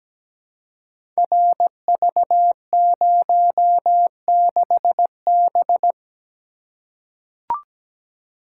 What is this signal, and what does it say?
Morse code, heard as a keyed tone, about 17 wpm, 700 Hz. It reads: RV06B
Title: Code Groups - 5 Letter:Numbers - 17wpm